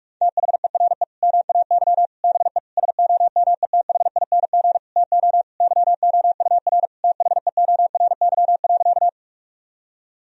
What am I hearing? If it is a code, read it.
Morse code, heard as a keyed tone, about 46 wpm, 700 Hz. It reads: THERE MAY BE SOMETHING TO YOUR THEORY.
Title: Sentences from Top 1000 Words - Review Part 3 - 45wpm